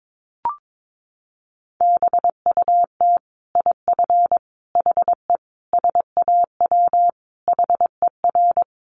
Morse code, heard as a keyed tone, about 22 wpm, 700 Hz. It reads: BUT IF HE SAW HER
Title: Sentences from Top 1400 Words 22wpm